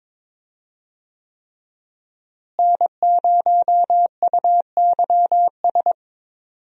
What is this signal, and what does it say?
Morse code, heard as a keyed tone, about 21 wpm, 700 Hz. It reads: N0UYS
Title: USA Call Signs - 5 Characters - Repeated 3x - 22wpm